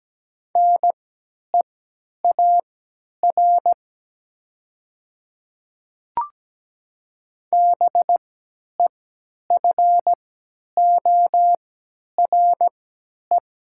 Morse code, heard as a keyed tone, about 17 wpm, spaced as for 10 wpm, 700 Hz. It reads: NEAR BEFORE
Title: 3x Character Spacing - Top 300 Words - 17wpm